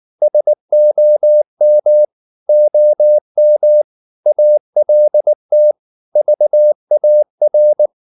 Morse code, heard as a keyed tone, about 19 wpm, 600 Hz. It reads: SOM OM ALT VAR
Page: Ferie 19 wpm